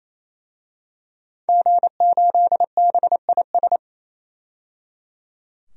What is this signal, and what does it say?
Morse code, heard as a keyed tone, about 28 wpm, 700 Hz. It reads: G8BIS
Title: DXCC-100 Call Signs - 5 Characters - 28wpm